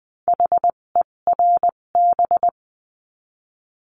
Morse code, heard as a keyed tone, about 20 wpm, spaced as for 17 wpm, 700 Hz. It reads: HERB
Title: 257 Lesson 42b - Words 14-20wpm